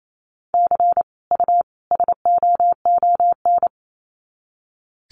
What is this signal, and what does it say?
Morse code, heard as a keyed tone, about 28 wpm, 700 Hz. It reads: C U SOON
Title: Common QSO Phrases 28wpm